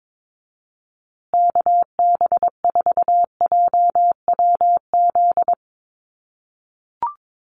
Morse code, heard as a keyed tone, about 22 wpm, 700 Hz. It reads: KB4JWZ
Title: USA Call Signs Repeated 2x 22wpm